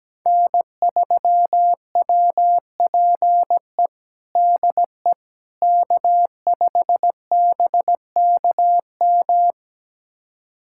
Morse code, heard as a keyed tone, about 17 wpm, 700 Hz. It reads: N3WPE DE K5BKM
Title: USA Callsign DE Callsign - 5 Characters - 17wpm